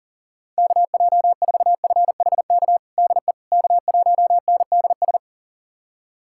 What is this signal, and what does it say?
Morse code, heard as a keyed tone, about 40 wpm, 700 Hz. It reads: KJ4FHK DE K1NDS
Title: USA Callsign DE Callsign 40wpm